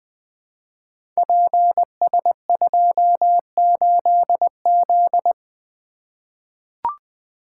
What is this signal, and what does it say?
Morse code, heard as a keyed tone, about 20 wpm, 700 Hz. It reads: PS28Z
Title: Code Groups - 5 Letter:Numbers - 20wpm